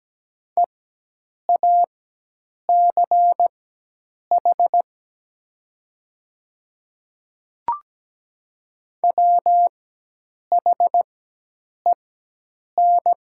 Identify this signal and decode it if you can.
Morse code, heard as a keyed tone, about 17 wpm, spaced as for 8 wpm, 700 Hz. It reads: EACH WHEN
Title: 4x Character Spacing - Top 100 Words - 17wpm